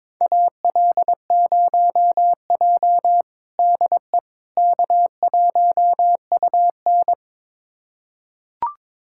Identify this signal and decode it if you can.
Morse code, heard as a keyed tone, about 22 wpm, 700 Hz. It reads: AL0J DE K1UN
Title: USA Callsign DE Callsign - 4 Characters - 22wpm